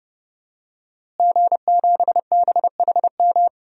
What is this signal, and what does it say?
Morse code, heard as a keyed tone, about 30 wpm, 700 Hz. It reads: G7BHM
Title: DXCC-100 Call Signs - 5 Characters - Repeated 3x - 30wpm